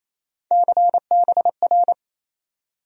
Morse code, heard as a keyed tone, about 28 wpm, 700 Hz. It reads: CBR
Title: USA Call Sign Postfixes - 3 Letters 28wpm